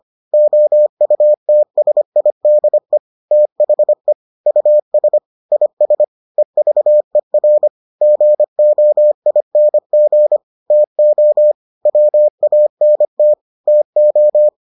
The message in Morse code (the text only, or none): OUTSIDE THE US IS EVER GOING TO WANT TO